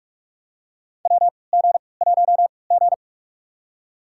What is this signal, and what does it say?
Morse code, heard as a keyed tone, about 46 wpm, spaced as for 26 wpm, 700 Hz. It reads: WG1G
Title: USA Call Signs - 3x Character Spacing - 45wpm